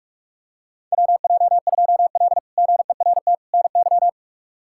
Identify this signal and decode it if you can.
Morse code, heard as a keyed tone, about 47 wpm, 700 Hz. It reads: WJ2L GERT NY
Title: K1USN SST QSO Practice - 45wpm